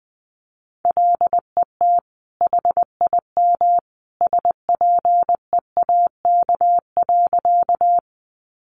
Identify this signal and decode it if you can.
Morse code, heard as a keyed tone, about 20 wpm, 700 Hz. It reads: LET HIM SPEAK.
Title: Sentences from Top 1100 Words 20wpm